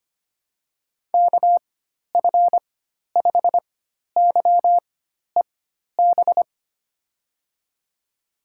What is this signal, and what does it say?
Morse code, heard as a keyed tone, about 25 wpm, spaced as for 12 wpm, 700 Hz. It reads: KF5YEB
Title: USA Call Signs - 6 Characters - 4x Character Spacing - 25wpm